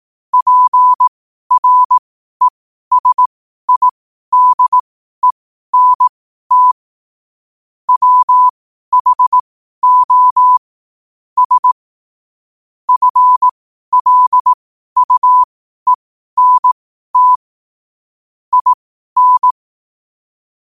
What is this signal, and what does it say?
Morse code, heard as a keyed tone, about 18 wpm, spaced as for 12 wpm, 1000 Hz. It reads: PRESIDENT WHO S FLUENT IN
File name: qotdfiles / 12wpm